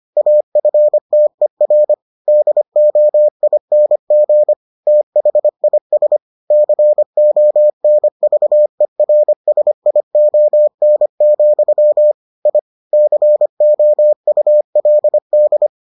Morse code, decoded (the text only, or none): AFTER DOING THIS CONVERSION, I COULD